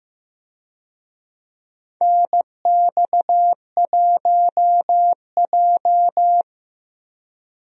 Morse code, heard as a keyed tone, about 15 wpm, 700 Hz. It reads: NX1J
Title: USA Call Signs - 4 Characters - 15wpm